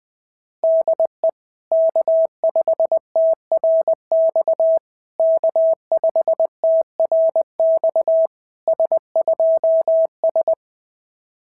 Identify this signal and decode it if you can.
Morse code, heard as a keyed tone, about 19 wpm, 650 Hz. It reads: DE K5TRX K5TRX S2S
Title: SOTA QSO Practice 20wpm